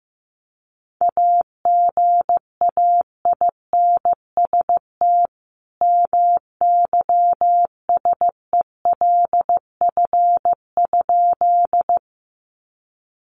Morse code, heard as a keyed tone, about 14 wpm, 700 Hz. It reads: AGAINST MYSELF?
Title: Sentences from Top 1300 Words 15wpm